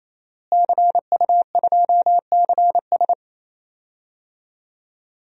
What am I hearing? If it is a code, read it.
Morse code, heard as a keyed tone, about 28 wpm, 700 Hz. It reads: CU2CS
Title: DXCC-100 Call Signs - 5 Characters - Repeated 3x - 28wpm